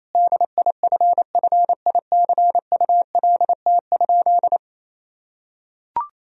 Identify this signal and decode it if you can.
Morse code, heard as a keyed tone, about 28 wpm, 700 Hz. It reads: DIFFICULT?
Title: Sentences from Top 1000 Words - Review Part 4 - 28wpm